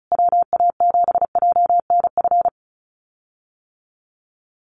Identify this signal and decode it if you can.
Morse code, heard as a keyed tone, about 34 wpm, 700 Hz. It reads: WA7JNF
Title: USA Call Signs Repeated 3x 35wpm